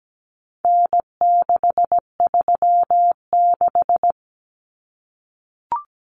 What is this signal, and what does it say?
Morse code, heard as a keyed tone, about 17 wpm, 700 Hz. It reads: N636
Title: Common Cut Numbers - 4-Digit 17wpm